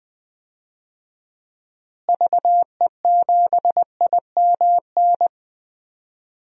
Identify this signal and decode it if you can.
Morse code, heard as a keyed tone, about 20 wpm, 700 Hz. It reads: VE7IMN
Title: DXCC-100 Call Signs - 6 Characters - Repeated 2x - 20wpm